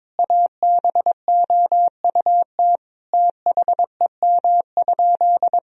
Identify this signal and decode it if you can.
Morse code, heard as a keyed tone, about 22 wpm, 700 Hz. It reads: ABOUT THEM?
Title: Sentences from Top 1000 Words - Review Part 1 - 22wpm